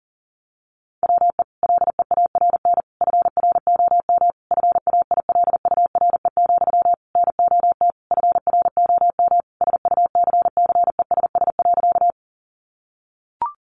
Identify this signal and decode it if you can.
Morse code, heard as a keyed tone, about 40 wpm, 700 Hz. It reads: WE LEARN FROM FAILURE, NOT FROM SUCCESS.
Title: Sentences from Top 900 Words 40wpm